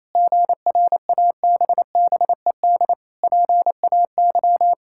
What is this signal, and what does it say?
Morse code, heard as a keyed tone, about 27 wpm, 700 Hz. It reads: GRABBED PAY
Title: Pangram Challenge v2 - 28wpm